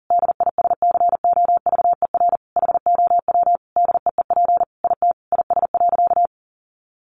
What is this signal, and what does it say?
Morse code, heard as a keyed tone, about 40 wpm, 700 Hz. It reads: DISCOVER HOW DEEP IT IS.
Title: Sentences from Top 1100 Words 40wpm